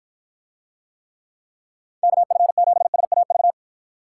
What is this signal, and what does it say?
Morse code, heard as a keyed tone, about 50 wpm, 700 Hz. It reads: KF7SAV